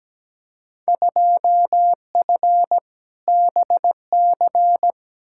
Morse code, heard as a keyed tone, about 17 wpm, 700 Hz. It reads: 2F BC
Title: Field Day Exchanges Encore - 17wpm